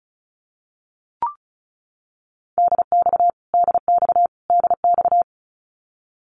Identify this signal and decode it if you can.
Morse code, heard as a keyed tone, about 34 wpm, 700 Hz. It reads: DX DX DX
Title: Abbreviations Repeated 3x - 35wpm